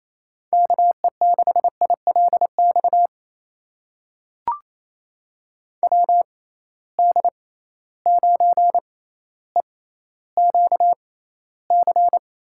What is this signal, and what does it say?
Morse code, heard as a keyed tone, about 28 wpm, spaced as for 14 wpm, 700 Hz. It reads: KE6ILX WD9EQC
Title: USA Call Signs - 6 Characters - 6x Character Spacing - 28wpm